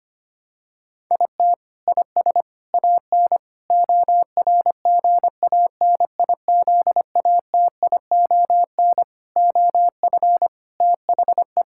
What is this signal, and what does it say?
Morse code, heard as a keyed tone, about 25 wpm, 700 Hz. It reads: IT IS AN ORGANIZATION OF THE
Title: Sentences from Top 1000 Words - Review Part 6 - 25wpm